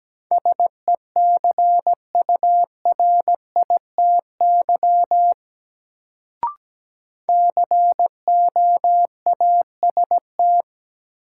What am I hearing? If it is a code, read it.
Morse code, heard as a keyed tone, about 17 wpm, 700 Hz. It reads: SECURITY COAST
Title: Sentences from Top 1000 Words - Review Part 4 - New Word Drill - 17wpm